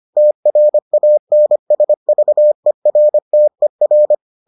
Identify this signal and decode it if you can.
Morse code, heard as a keyed tone, about 25 wpm, 600 Hz. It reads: TRANSVERTER